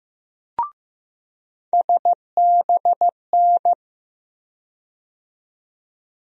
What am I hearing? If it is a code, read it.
Morse code, heard as a keyed tone, about 16 wpm, 700 Hz. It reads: SBN
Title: USA Call Sign Postfixes - 3 Letters 15wpm